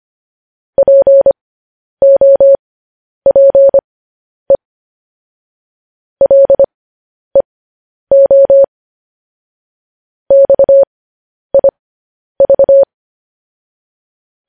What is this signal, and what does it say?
Morse code, heard as a keyed tone, about 25 wpm, spaced as for 10 wpm, 550 Hz. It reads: POPE LEO XIV
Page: news-headlines-in-morse-code-at-10-wpm